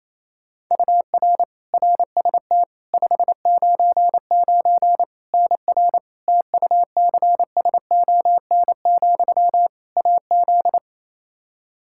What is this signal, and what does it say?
Morse code, heard as a keyed tone, about 28 wpm, 700 Hz. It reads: UR RST 599 NR TUCSON, AZ
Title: Practice QSO - Exchange 1 - Beginning - 28wpm